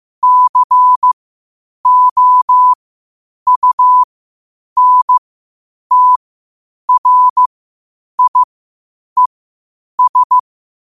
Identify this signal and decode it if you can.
Morse code, heard as a keyed tone, about 15 wpm, spaced as for 8 wpm, 1000 Hz. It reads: COUNTRIES
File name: qotdfiles / 8wpm